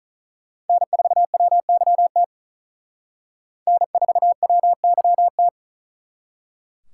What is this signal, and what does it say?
Morse code, heard as a keyed tone, about 44 wpm, 700 Hz. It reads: N4WYT N4WYT